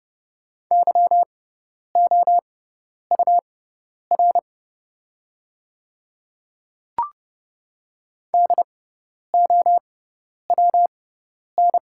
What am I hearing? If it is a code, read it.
Morse code, heard as a keyed tone, about 30 wpm, spaced as for 10 wpm, 700 Hz. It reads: YOUR DOWN
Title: 6x Character Spacing - Top 100 Words - 30wpm